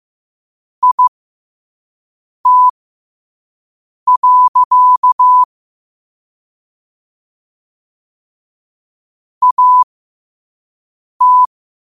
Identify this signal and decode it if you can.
Morse code, heard as a keyed tone, about 15 wpm, spaced as for 5 wpm, 1000 Hz. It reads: IT. AT
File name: qotdfiles / 5wpm